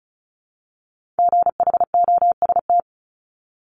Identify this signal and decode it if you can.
Morse code, heard as a keyed tone, about 36 wpm, 700 Hz. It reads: GHOST
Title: Multi-letter Phonemes 35wpm